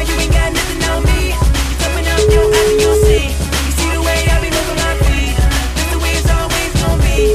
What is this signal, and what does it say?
Ringback tone (Europe, 425 Hz), with noise.